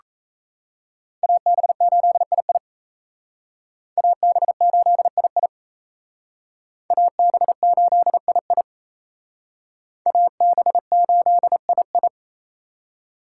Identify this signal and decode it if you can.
Morse code, heard as a keyed tone, about 36 wpm, 700 Hz. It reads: AB8II AB8II AB8II AB8II